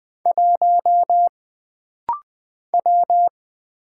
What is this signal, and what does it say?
Morse code, heard as a keyed tone, about 20 wpm, 700 Hz. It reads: 1 W
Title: Single Letter-Number - Rapid-Fire 20wpm